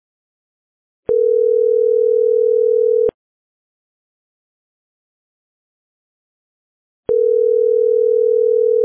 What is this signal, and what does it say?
Ringback tone (North America), 440 + 480 Hz.